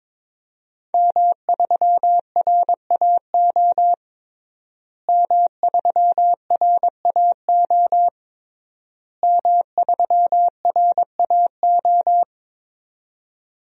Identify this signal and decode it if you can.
Morse code, heard as a keyed tone, about 21 wpm, 700 Hz. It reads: M3RAO M3RAO M3RAO
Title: DXCC-100 Call Signs - 5 Characters - Repeated 3x - 22wpm